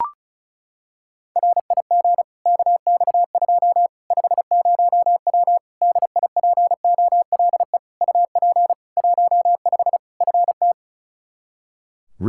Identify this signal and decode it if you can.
Morse code, heard as a keyed tone, about 36 wpm, 700 Hz. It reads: RIG KX2 50W DIPOLE UP 15 FT
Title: K1USN SST QSO Practice 35wpm